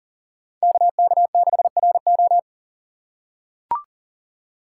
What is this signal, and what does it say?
Morse code, heard as a keyed tone, about 40 wpm, 700 Hz. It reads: KK6RO
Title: USA Call Signs - 5 Characters - 3x Character Spacing - 40wpm